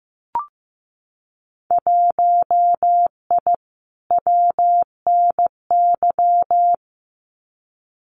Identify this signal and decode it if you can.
Morse code, heard as a keyed tone, about 15 wpm, 700 Hz. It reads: 1I WNY
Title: Winter Field Day Exchanges Encore 15wpm